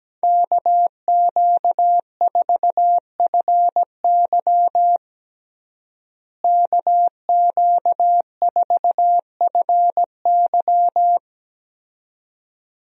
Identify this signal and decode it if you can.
Morse code, heard as a keyed tone, about 17 wpm, 700 Hz. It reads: KQ4FY KQ4FY
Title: USA Call Signs - 5 Characters - Repeated 2x - 17wpm